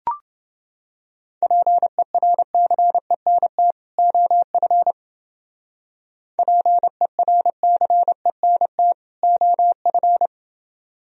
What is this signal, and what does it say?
Morse code, heard as a keyed tone, about 29 wpm, 700 Hz. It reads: PERCENT OF PERCENT OF